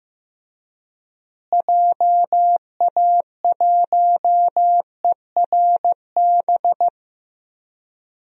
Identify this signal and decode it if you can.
Morse code, heard as a keyed tone, about 15 wpm, 700 Hz. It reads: JA1ERB
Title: DXCC-100 Call Signs - 6 Characters - Repeated 3x - 15wpm